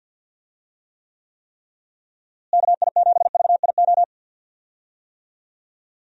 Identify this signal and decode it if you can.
Morse code, heard as a keyed tone, about 46 wpm, 700 Hz. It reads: KI7VIO
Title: USA Call Signs - 6 Characters - Repeated 3x - 50wpm